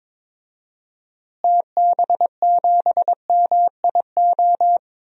Morse code, heard as a keyed tone, about 22 wpm, 700 Hz. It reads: TB7MIO
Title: DXCC-100 Call Signs - 6 Characters - Repeated 2x - 22wpm